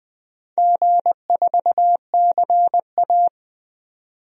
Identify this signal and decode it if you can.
Morse code, heard as a keyed tone, about 20 wpm, 700 Hz. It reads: G4CA
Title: DXCC-100 Call Signs - 4 Characters - 20wpm